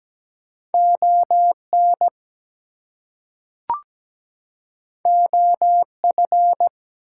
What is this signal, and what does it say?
Morse code, heard as a keyed tone, about 17 wpm, 700 Hz. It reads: ON OF
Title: Two Letter Words - 17wpm